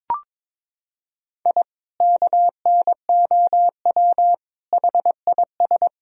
Morse code, heard as a keyed tone, about 22 wpm, 700 Hz. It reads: I KNOW HIS
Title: Sentences from Top 1000 Words - Review Part 3 - 22wpm